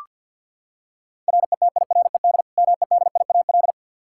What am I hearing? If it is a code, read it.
Morse code, heard as a keyed tone, about 47 wpm, 700 Hz. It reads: RETIRED MEDIAL
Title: Practice QSO - Exchange 3 - More Info - 50wpm